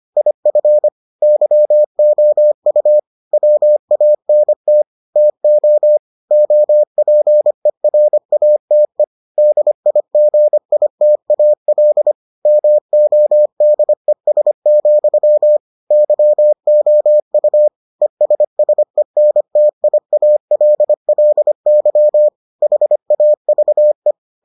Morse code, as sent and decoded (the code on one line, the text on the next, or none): .. ..-. / -.-- --- ..- / .-- .- -. - / - --- / --- .--. .-. .- - . / -.. .. --. .. - .- .-.. / -- --- -.. ... --..-- / -.-- --- ..- / . ... ... -. - .. .- .-.. .-.. -.-- / .... .- ...- .
IF YOU WANT TO OPERATE DIGITAL MODES, YOU ESSENTIALLY HAVE